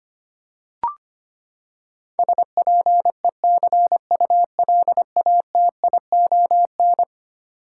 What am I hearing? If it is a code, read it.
Morse code, heard as a keyed tone, about 25 wpm, 700 Hz. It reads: SPECULATION
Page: morse-code-ninja-practice-25wpm